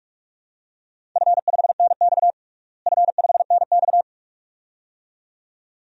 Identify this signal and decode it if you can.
Morse code, heard as a keyed tone, about 46 wpm, 700 Hz. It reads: F5NX F5NX
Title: DXCC-100 Call Signs - 4 Characters - Repeated 2x - 45wpm